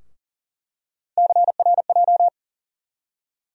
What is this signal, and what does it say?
Morse code, heard as a keyed tone, about 40 wpm, 700 Hz. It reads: CRJ
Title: USA Call Sign Postfixes - 3 Letters 40wpm